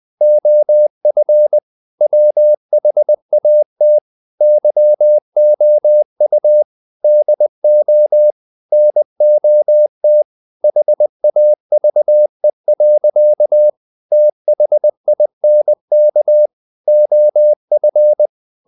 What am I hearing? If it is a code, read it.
Morse code, heard as a keyed tone, about 20 wpm, 600 Hz. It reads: OF WHAT YOU DO NOT HAVE. THINK OF
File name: the old man and the sea part 2 20wpm